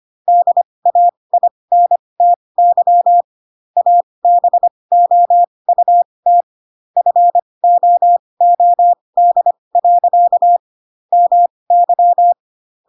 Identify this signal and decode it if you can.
Morse code, heard as a keyed tone, about 25 wpm, spaced as for 20 wpm, 700 Hz. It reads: DAINTY ABOUT FOOD. MY
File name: Pinocchio / Pinocchio ch7 15 25 ik1hge